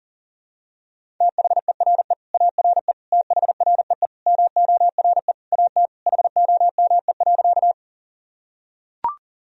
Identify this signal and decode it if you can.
Morse code, heard as a keyed tone, about 40 wpm, 700 Hz. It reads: THERE ARE THREE MORE AT HOME.
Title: Sentences from Top 1000 Words - Review Part 1 - 40wpm